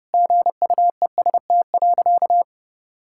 Morse code, heard as a keyed tone, about 30 wpm, 700 Hz. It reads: GUEST.
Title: Sentences from Top 1600 Words 30wpm